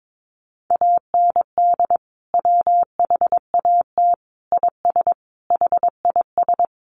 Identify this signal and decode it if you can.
Morse code, heard as a keyed tone, about 22 wpm, 700 Hz. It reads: AND WHAT IS HIS
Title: Sentences from Top 800 Words 22wpm